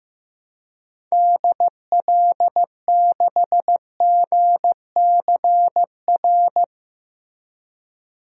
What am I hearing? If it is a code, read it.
Morse code, heard as a keyed tone, about 15 wpm, 700 Hz. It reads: DL6GCR
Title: DXCC-100 Call Signs - 6 Characters - Repeated 3x - 15wpm